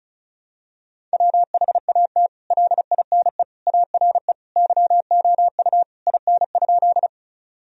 Morse code, heard as a keyed tone, about 36 wpm, 700 Hz. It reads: WHAT LINE ARE YOU IN?
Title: Sentences from Top 1000 Words - Review Part 1 - 35wpm